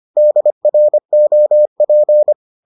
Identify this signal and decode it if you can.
Morse code, heard as a keyed tone, about 25 wpm, 600 Hz. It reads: DROP